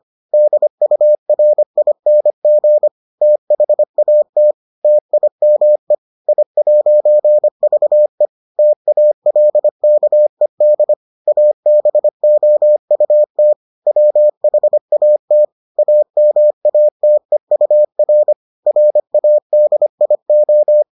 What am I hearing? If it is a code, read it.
Morse code, heard as a keyed tone, about 25 wpm, 600 Hz. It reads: DURING THAT TIME I'VE TALKED ABOUT WHAT AMATEUR RADIO